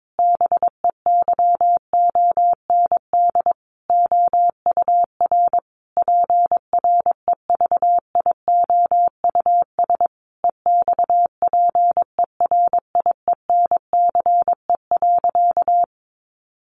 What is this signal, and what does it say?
Morse code, heard as a keyed tone, about 22 wpm, 700 Hz. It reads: BEYOND OUR PREVIOUS EXPERIENCE.
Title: Sentences from Top 1500 Words 22wpm